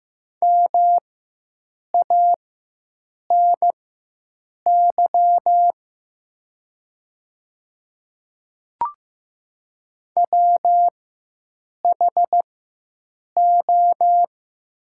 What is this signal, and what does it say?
Morse code, heard as a keyed tone, about 15 wpm, spaced as for 7 wpm, 690 Hz. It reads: MANY WHO
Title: 4x Character Spacing - Top 200 Words - 15wpm